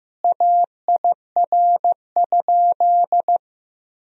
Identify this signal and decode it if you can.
Morse code, heard as a keyed tone, about 16 wpm, 700 Hz. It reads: AIR?
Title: Sentences from Top 1000 Words - Review Part 1 - 15wpm